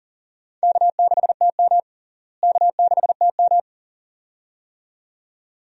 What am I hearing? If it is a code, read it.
Morse code, heard as a keyed tone, about 40 wpm, 700 Hz. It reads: K6TM K6TM
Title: USA Call Signs - 4 Characters - Repeated 2x - 40wpm